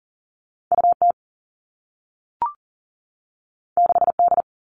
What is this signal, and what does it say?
Morse code, heard as a keyed tone, about 40 wpm, 700 Hz. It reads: U T 6 D
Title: Two Challenging Letter-Number Combinations 40wpm